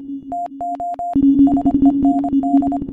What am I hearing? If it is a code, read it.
Morse code, heard as a keyed tone, about 25 wpm, 700 Hz. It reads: TO SEND
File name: MorseCodeChoir